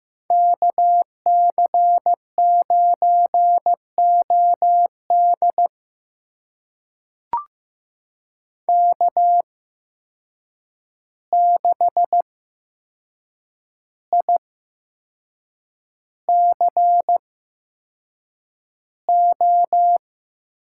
Morse code, heard as a keyed tone, about 15 wpm, spaced as for 6 wpm, 700 Hz. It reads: KC9OD K6ICO
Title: USA Call Signs - 5 Characters - 8x Character Spacing - 15wpm